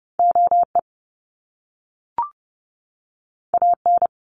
Morse code, heard as a keyed tone, about 30 wpm, 700 Hz. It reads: OE AN
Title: Top Two Letter Combinations in Words 30wpm